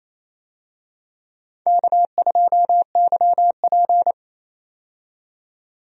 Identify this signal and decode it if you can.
Morse code, heard as a keyed tone, about 28 wpm, 700 Hz. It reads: K2YP
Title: USA Call Signs - 4 Characters - 28wpm